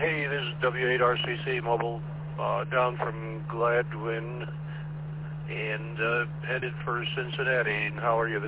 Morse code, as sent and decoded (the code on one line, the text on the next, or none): none